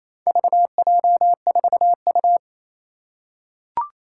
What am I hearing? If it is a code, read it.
Morse code, heard as a keyed tone, about 28 wpm, 700 Hz. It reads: VJ4U
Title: Four Challenging Letter-Number Combinations 28wpm